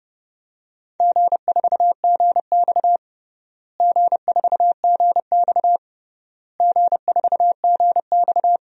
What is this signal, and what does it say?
Morse code, heard as a keyed tone, about 30 wpm, 700 Hz. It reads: G4GX G4GX G4GX
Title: DXCC-100 Call Signs - 4 Characters - Repeated 3x - 30wpm